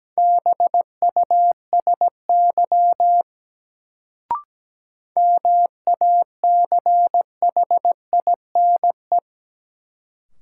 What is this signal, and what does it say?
Morse code, heard as a keyed tone, about 17 wpm, 700 Hz. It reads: BUSY MACHINE
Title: Sentences from Top 1000 Words - Review Part 2 - New Word Drill - 17wpm